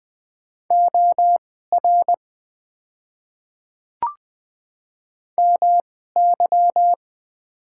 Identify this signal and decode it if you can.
Morse code, heard as a keyed tone, about 20 wpm, spaced as for 14 wpm, 700 Hz. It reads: OR MY
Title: 2x Character Spacing - Top 100 Words - 20wpm